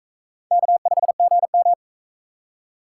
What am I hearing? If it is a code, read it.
Morse code, heard as a keyed tone, about 43 wpm, 700 Hz. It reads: K5GM